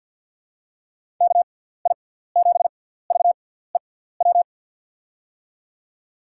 Morse code, heard as a keyed tone, about 46 wpm, spaced as for 17 wpm, 700 Hz. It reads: KI7VEW
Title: USA Call Signs - 6x Character Spacing - 50wpm